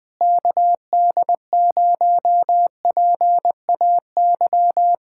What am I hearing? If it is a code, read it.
Morse code, heard as a keyed tone, about 20 wpm, 700 Hz. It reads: KD0PAY
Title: Speed-Racing Common QSO Phrases - 20wpm